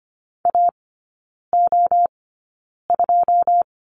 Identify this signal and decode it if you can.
Morse code, heard as a keyed tone, about 25 wpm, spaced as for 9 wpm, 690 Hz. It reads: AO2
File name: Lesson 14e - Number 2 - Three Characters 25wpm